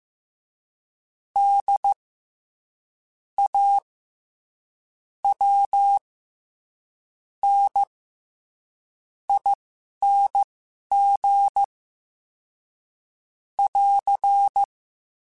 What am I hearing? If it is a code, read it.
Morse code, heard as a keyed tone, about 15 wpm, spaced as for 11 wpm, 800 Hz. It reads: D A W N ING <AR>
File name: Special INGs-mp3s / Special INGs-mp3s / SlowToFaster ing @10